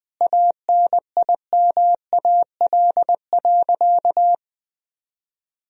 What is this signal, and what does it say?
Morse code, heard as a keyed tone, about 20 wpm, 700 Hz. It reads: ANIMAL.
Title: Sentences from Top 1000 Words - Review Part 1 - 20wpm